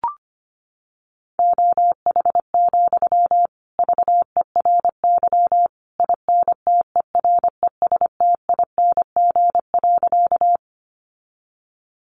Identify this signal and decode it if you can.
Morse code, heard as a keyed tone, about 25 wpm, 700 Hz. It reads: OH, VERY INTERESTING.
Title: Sentences from Top 1000 Words 25wpm